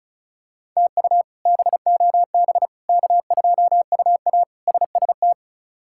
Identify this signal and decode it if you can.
Morse code, heard as a keyed tone, about 34 wpm, 700 Hz. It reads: TU BOB K2UA SST
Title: K1USN SST QSO Practice - 35wpm